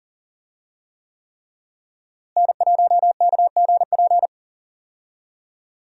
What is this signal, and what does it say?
Morse code, heard as a keyed tone, about 40 wpm, 700 Hz. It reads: N1KGP
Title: USA Call Signs - 5 Characters - Repeated 3x - 40wpm